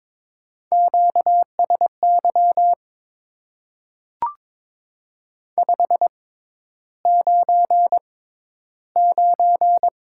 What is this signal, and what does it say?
Morse code, heard as a keyed tone, about 22 wpm, spaced as for 11 wpm, 700 Hz. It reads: QSY 599
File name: Top 90 QSO Elements - 6x Character Spacing - 22wpm